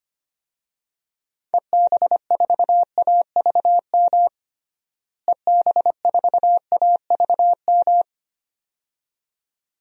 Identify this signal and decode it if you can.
Morse code, heard as a keyed tone, about 25 wpm, 700 Hz. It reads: EB4AVM EB4AVM
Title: DXCC-100 Call Signs - 6 Characters - Repeated 2x - 25wpm